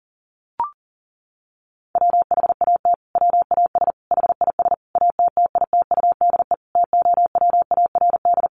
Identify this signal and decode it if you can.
Morse code, heard as a keyed tone, about 40 wpm, 700 Hz. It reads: WHAT WAS HIS ATTITUDE TOWARD
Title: Sentences from Top 1300 Words 40wpm